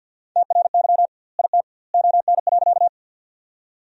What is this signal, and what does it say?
Morse code, heard as a keyed tone, about 52 wpm, 700 Hz. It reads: TRY IT ON.
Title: 2x Word-Spacing - Sentences from Top 200 Words - 50wpm